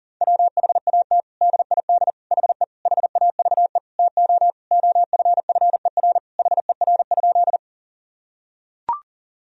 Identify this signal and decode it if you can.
Morse code, heard as a keyed tone, about 40 wpm, 700 Hz. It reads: WHAT DID HE HAVE TO OFFER HER?
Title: Sentences from Top 1000 Words - Review Part 4 - 40wpm